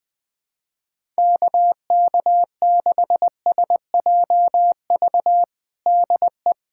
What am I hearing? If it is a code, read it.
Morse code, heard as a keyed tone, about 20 wpm, 700 Hz. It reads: KK6SJV DE
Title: USA Callsign DE Callsign - 6 Characters - 20wpm